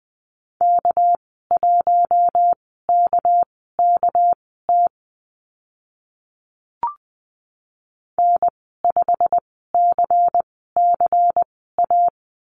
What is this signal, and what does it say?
Morse code, heard as a keyed tone, about 20 wpm, spaced as for 14 wpm, 700 Hz. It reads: K1KKT N5CCA
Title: 231 Lesson 37g - Letter x - Call Signs 20wpm